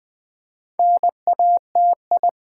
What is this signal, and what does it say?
Morse code, heard as a keyed tone, about 20 wpm, 700 Hz. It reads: NATI
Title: Build A Word - 2 Letters At A Time - 8-Letter Words - 20wpm